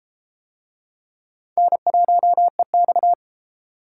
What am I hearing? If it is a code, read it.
Morse code, heard as a keyed tone, about 30 wpm, 700 Hz. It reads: N1EX